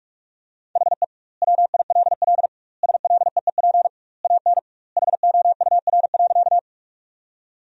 Morse code, heard as a keyed tone, about 46 wpm, 700 Hz. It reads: HE WILL SLEEP AN HOUR.
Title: 2x Word-Spacing - Sentences from Top 500 Words - 45wpm